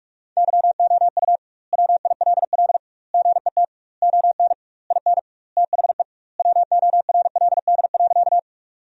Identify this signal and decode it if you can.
Morse code, heard as a keyed tone, about 46 wpm, 700 Hz. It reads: YOU WILL GET ON IN THE WORLD.
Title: 2x Word-Spacing - Sentences from Top 200 Words - 45wpm